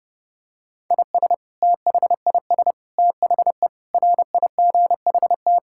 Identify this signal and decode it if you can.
Morse code, heard as a keyed tone, about 32 wpm, 700 Hz. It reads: IS THIS THE RIGHT
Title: Sentences from Top 1000 Words - Review Part 6 - 30wpm